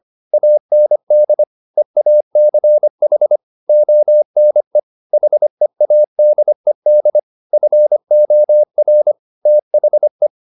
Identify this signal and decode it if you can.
Morse code, heard as a keyed tone, about 25 wpm, 600 Hz. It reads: AND EACH ONE HEADED FOR THE
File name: the old man and the sea part 1 25wpm